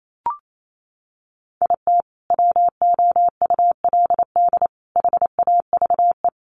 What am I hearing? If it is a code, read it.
Morse code, heard as a keyed tone, about 28 wpm, 700 Hz. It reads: IT WOULD HAVE
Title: Sentences from Top 1100 Words 28wpm